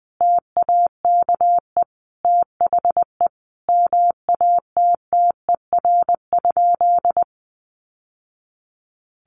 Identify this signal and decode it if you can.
Morse code, heard as a keyed tone, about 20 wpm, 700 Hz. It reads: TAKE THE MATTER?
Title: Sentences from Top 700 Words 20wpm